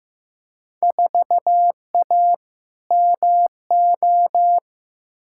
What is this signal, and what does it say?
Morse code, heard as a keyed tone, about 15 wpm, 700 Hz. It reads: 4A MO
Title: Field Day Exchanges - 15wpm